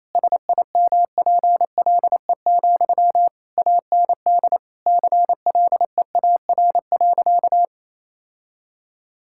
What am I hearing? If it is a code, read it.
Morse code, heard as a keyed tone, about 28 wpm, 700 Hz. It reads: SIMPLE, AND CLEAR.
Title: Sentences from Top 1000 Words - Review Part 6 - 28wpm